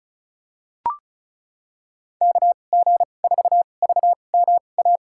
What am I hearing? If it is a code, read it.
Morse code, heard as a keyed tone, about 36 wpm, spaced as for 25 wpm, 700 Hz. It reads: KG4VMA
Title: USA Call Signs - 6 Characters - 2x Character Spacing - 35wpm